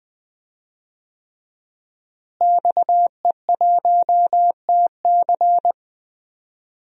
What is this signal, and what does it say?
Morse code, heard as a keyed tone, about 20 wpm, 700 Hz. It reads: XE1TC
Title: DXCC-100 Call Signs - 5 Characters - Repeated 3x - 20wpm